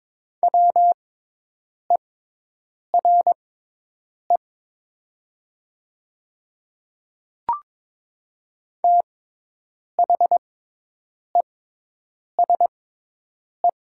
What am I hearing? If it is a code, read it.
Morse code, heard as a keyed tone, about 22 wpm, spaced as for 8 wpm, 700 Hz. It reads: WERE THESE
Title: 6x Character Spacing - Top 100 Words - 22wpm